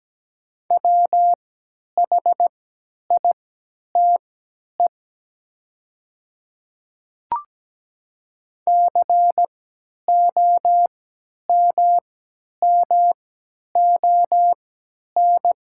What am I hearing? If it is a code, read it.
Morse code, heard as a keyed tone, about 17 wpm, spaced as for 10 wpm, 700 Hz. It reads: WHITE COMMON
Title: 3x Character Spacing - Top 500 Words - 17wpm